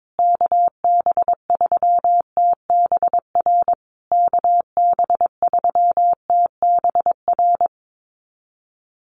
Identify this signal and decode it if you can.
Morse code, heard as a keyed tone, about 22 wpm, 700 Hz. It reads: KB3TBR KB3TBR
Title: USA Call Signs Repeated 3x 22wpm